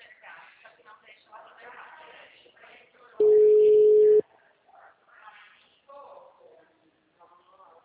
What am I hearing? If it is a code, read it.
Ringback tone (Europe, 425 Hz), with speech.
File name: out- -2001- - - .32